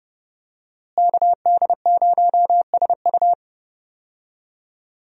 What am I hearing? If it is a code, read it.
Morse code, heard as a keyed tone, about 30 wpm, 700 Hz. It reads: KD0SU
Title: USA Call Signs - 5 Characters - Repeated 2x - 30wpm